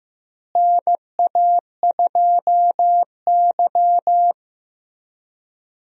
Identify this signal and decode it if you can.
Morse code, heard as a keyed tone, about 15 wpm, 700 Hz. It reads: NA2Y
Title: USA Call Signs - 4 Characters - Repeated 3x - 15wpm